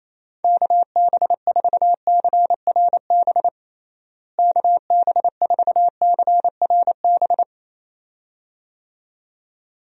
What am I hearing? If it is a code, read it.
Morse code, heard as a keyed tone, about 27 wpm, 700 Hz. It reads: KB4CRB KB4CRB
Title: USA Call Signs - 6 Characters - Repeated 2x - 28wpm